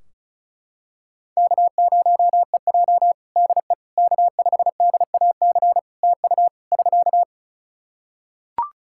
Morse code, heard as a keyed tone, about 36 wpm, 700 Hz. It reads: K0EJ DE K5DAC TU <SK>
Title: Practice QSO - Exchange 4 - Ending - 35wpm